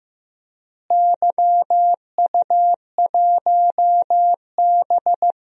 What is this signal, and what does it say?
Morse code, heard as a keyed tone, about 15 wpm, 700 Hz. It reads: YU1B
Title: DXCC-100 Call Signs - 4 Characters - Repeated 3x - 15wpm